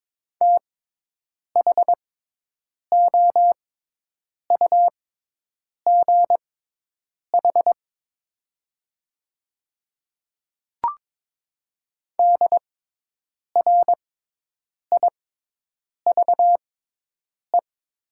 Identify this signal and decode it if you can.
Morse code, heard as a keyed tone, about 22 wpm, spaced as for 8 wpm, 700 Hz. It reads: THOUGH DRIVE
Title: 6x Character Spacing - Top 500 Words - 22wpm